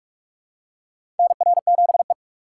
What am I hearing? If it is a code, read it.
Morse code, heard as a keyed tone, about 43 wpm, 700 Hz. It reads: NR7E
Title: USA Call Signs - 4 Characters - 6x Character Spacing - 45wpm